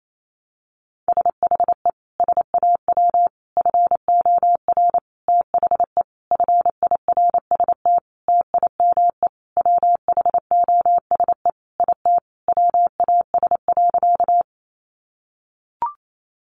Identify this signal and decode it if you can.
Morse code, heard as a keyed tone, about 28 wpm, 700 Hz. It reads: SHE SAW FOR THE FIRST TIME WHOSE IT WAS.
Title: Sentences from Top 1200 Words 28wpm